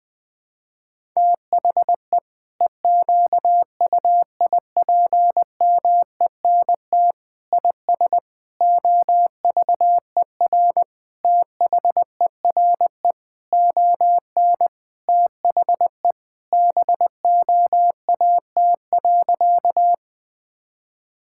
Morse code, heard as a keyed tone, about 20 wpm, 700 Hz. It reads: THE EQUIPMENT IS OVER THERE ON THE BOAT.
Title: Sentences from Top 1000 Words - Review Part 5 - 20wpm